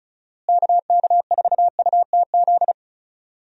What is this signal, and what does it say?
Morse code, heard as a keyed tone, about 36 wpm, 700 Hz. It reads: KK4UTZ
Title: USA Call Signs - 6 Characters - Repeated 2x - 35wpm